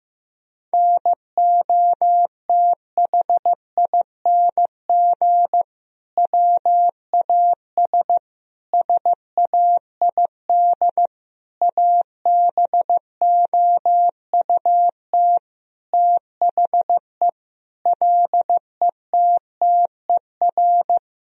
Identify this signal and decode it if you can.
Morse code, heard as a keyed tone, about 15 wpm, 700 Hz. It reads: NOTHING WAS SAID ABOUT THE LETTER
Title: Sentences from Top 1000 Words - Review Part 2 - 15wpm